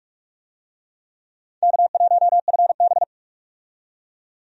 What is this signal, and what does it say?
Morse code, heard as a keyed tone, about 47 wpm, 700 Hz. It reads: K1FB
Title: USA Call Signs - 4 Characters - 45wpm